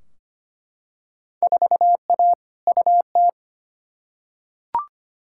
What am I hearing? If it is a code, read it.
Morse code, heard as a keyed tone, about 25 wpm, 700 Hz. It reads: 4A UT
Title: Field Day Exchanges - 25wpm